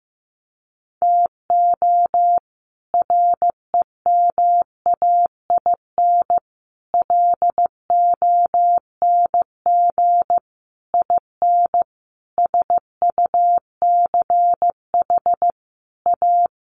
Morse code, heard as a keyed tone, about 14 wpm, 700 Hz. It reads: TO REMAIN LONG IN SUCH A
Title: Sentences from Top 1500 Words 15wpm